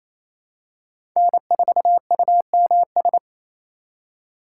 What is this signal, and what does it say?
Morse code, heard as a keyed tone, about 28 wpm, 700 Hz. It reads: N4UMS
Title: USA Call Signs - 5 Characters - 6x Character Spacing - 28wpm